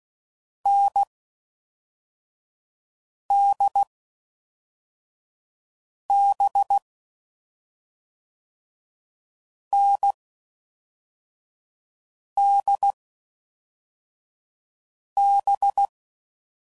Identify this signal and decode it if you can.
Morse code, heard as a keyed tone, about 16 wpm, spaced as for 4 wpm, 800 Hz. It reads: NDB NDB
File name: Combinations / Difficult Combos / NDB NDB6